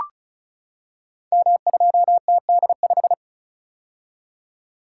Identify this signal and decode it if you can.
Morse code, heard as a keyed tone, about 36 wpm, 700 Hz. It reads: M2TD5
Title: Code Groups - 5 Letter:Numbers - 35wpm